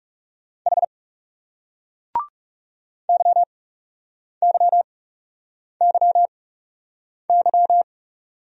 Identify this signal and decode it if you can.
Morse code, heard as a keyed tone, about 38 wpm, spaced as for 8 wpm, 700 Hz. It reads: H YYYY